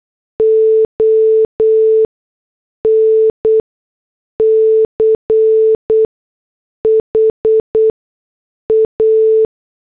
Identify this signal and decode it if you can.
Morse code, heard as a keyed tone, about 8 wpm, spaced as for 6 wpm, 440 Hz. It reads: ON CHA
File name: morse / headlines